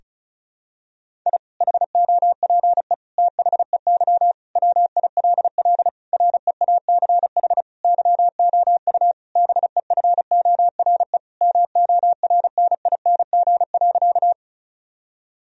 Morse code, heard as a keyed tone, about 36 wpm, 700 Hz. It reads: I HOPE THEY WILL REACH YOU BEFORE MORNING.
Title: Sentences from Top 1000 Words - Review Part 2 - 35wpm